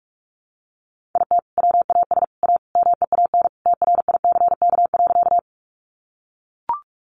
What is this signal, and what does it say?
Morse code, heard as a keyed tone, about 44 wpm, 700 Hz. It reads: IT WAS A MEAN TRICK.
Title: Sentences from Top 1500 Words 45wpm